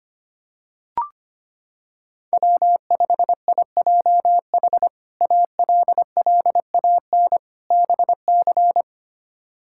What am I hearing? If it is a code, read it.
Morse code, heard as a keyed tone, about 25 wpm, 700 Hz. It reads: W5IJH ALLAN BC
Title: K1USN SST QSO Practice - 25wpm